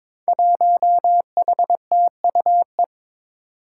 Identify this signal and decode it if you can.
Morse code, heard as a keyed tone, about 23 wpm, 700 Hz. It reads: 1HTUE
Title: Five Challenging Letter-Number Combinations 22wpm